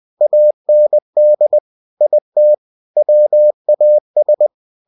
Morse code, heard as a keyed tone, about 20 wpm, 600 Hz. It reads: AND IT WAS